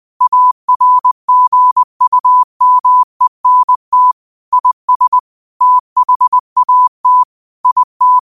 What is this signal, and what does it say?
Morse code, heard as a keyed tone, about 20 wpm, 1000 Hz. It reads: ARGUMENT IS THAT IT
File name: qotdfiles / 20wpm